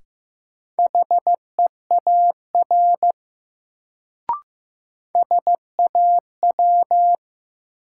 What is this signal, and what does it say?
Morse code, heard as a keyed tone, about 15 wpm, 700 Hz. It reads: HEAR SAW
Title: Sentences from Top 300 Words - New Word Drill - 15wpm